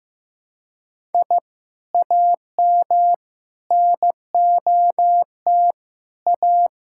Morse code, heard as a keyed tone, about 16 wpm, 700 Hz. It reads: I AM NOT A
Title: Sentences from Top 1000 Words - Review Part 6 - 15wpm